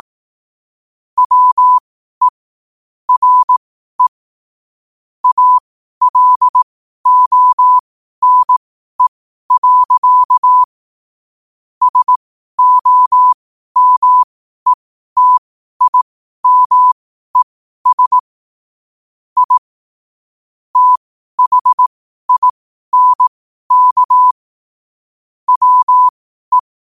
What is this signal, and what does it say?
Morse code, heard as a keyed tone, about 18 wpm, spaced as for 12 wpm, 1000 Hz. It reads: WE RE ALONE. SOMETIMES I THINK WE
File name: qotdfiles / 12wpm